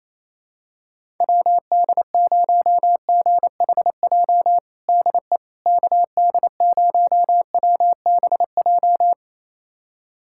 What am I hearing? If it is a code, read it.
Morse code, heard as a keyed tone, about 28 wpm, 700 Hz. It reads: WD0GHJ DE KD0WBJ
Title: USA Callsign DE Callsign - 6 Characters - 28wpm